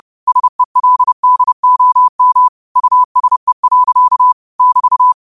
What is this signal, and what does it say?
Morse code, heard as a keyed tone, about 30 wpm, 1000 Hz. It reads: SELDOM USE. <BT>
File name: qotdfiles / 30wpm